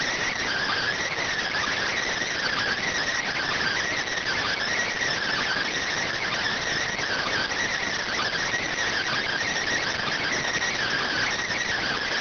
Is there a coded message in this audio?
Yes: Emergency Alert System (SAME) tones (header bursts) from 0.3 to 11.3 s, heard as harsh buzzing bursts.